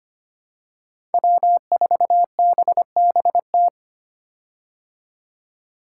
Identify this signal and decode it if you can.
Morse code, heard as a keyed tone, about 25 wpm, 700 Hz. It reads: W4BBT
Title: K1USN SST QSO Practice - 25wpm